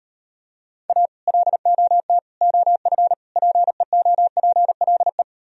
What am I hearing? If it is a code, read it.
Morse code, heard as a keyed tone, about 38 wpm, 700 Hz. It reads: A LOT OF PEOPLE